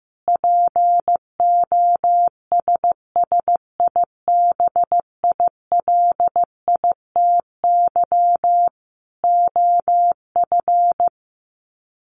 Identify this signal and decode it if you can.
Morse code, heard as a keyed tone, about 15 wpm, 700 Hz. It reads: POSSIBILITY OF
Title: Sets of 3 Words 15wpm